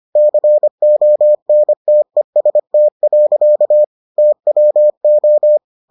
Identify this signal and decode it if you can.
Morse code, heard as a keyed tone, about 25 wpm, 600 Hz. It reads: CONTEST. TWO